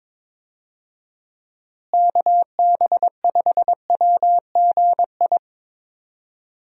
Morse code, heard as a keyed tone, about 22 wpm, 700 Hz. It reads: KB5WGI
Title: USA Call Signs - 6 Characters - Repeated 3x - 22wpm